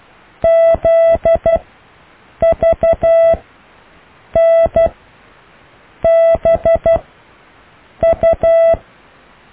Morse code, heard as a keyed tone, about 12 wpm, spaced as for 6 wpm, 660 Hz. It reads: ZVNBU
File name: cw / Alle letters